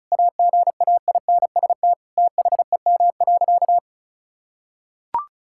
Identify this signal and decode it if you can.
Morse code, heard as a keyed tone, about 34 wpm, 700 Hz. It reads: AGAINST THEM.
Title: Sentences from Top 1000 Words - Review Part 6 - 35wpm